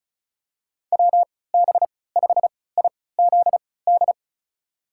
Morse code, heard as a keyed tone, about 34 wpm, spaced as for 20 wpm, 700 Hz. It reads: WB5IZD
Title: USA Call Signs - 6 Characters - 3x Character Spacing - 35wpm